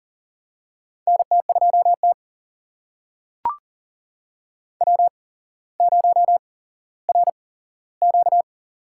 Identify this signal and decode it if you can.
Morse code, heard as a keyed tone, about 40 wpm, spaced as for 17 wpm, 700 Hz. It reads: NT2T W0RQ
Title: USA Call Signs - 8x Character Spacing - 40wpm